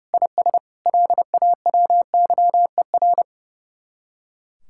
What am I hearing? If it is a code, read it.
Morse code, heard as a keyed tone, about 37 wpm, 700 Hz. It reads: IS LAWYER
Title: Speed-Racing Common QSO Phrases - 30wpm